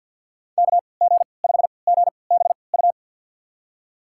Morse code, heard as a keyed tone, about 53 wpm, spaced as for 29 wpm, 700 Hz. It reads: KG5GBU
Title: USA Call Signs - 3x Character Spacing - 50wpm